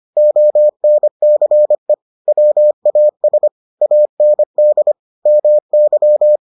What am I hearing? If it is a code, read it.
Morse code, heard as a keyed tone, about 25 wpm, 600 Hz. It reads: ONCE WAS AND MY